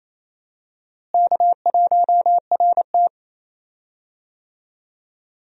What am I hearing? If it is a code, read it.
Morse code, heard as a keyed tone, about 28 wpm, 700 Hz. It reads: K1RT
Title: USA Call Signs - 4 Characters - Repeated 3x - 28wpm